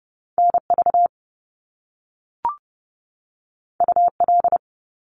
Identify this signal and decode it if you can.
Morse code, heard as a keyed tone, about 30 wpm, 700 Hz. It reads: NV UL
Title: Top Two Letter Combinations in Words 30wpm